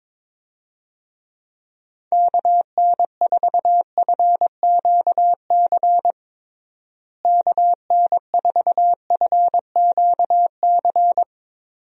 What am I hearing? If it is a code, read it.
Morse code, heard as a keyed tone, about 22 wpm, 700 Hz. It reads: KN4FQC KN4FQC
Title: USA Call Signs - 6 Characters - Repeated 2x - 22wpm